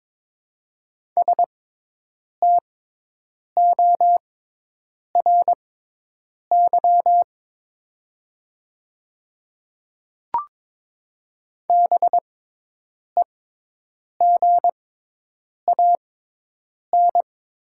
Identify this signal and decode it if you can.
Morse code, heard as a keyed tone, about 22 wpm, spaced as for 8 wpm, 700 Hz. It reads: STORY BEGAN
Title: 6x Character Spacing - Top 500 Words - 22wpm